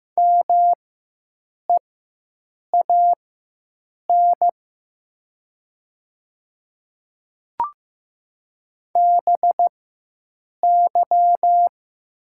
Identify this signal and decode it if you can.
Morse code, heard as a keyed tone, about 15 wpm, spaced as for 7 wpm, 700 Hz. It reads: MEAN BY
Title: 4x Character Spacing - Top 200 Words - 15wpm